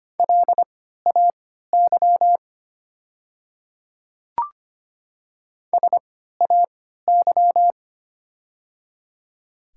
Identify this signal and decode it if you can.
Morse code, heard as a keyed tone, about 25 wpm, spaced as for 14 wpm, 700 Hz. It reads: LAY SAY
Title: 3x Character Spacing - Top 500 Words - 25wpm